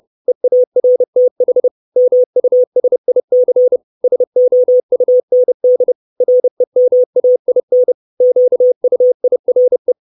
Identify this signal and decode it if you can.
Morse code, heard as a keyed tone, about 30 wpm, 500 Hz. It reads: EARTH MUSIC SOUND REMAIN QUIRE
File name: 632 Words 30WPM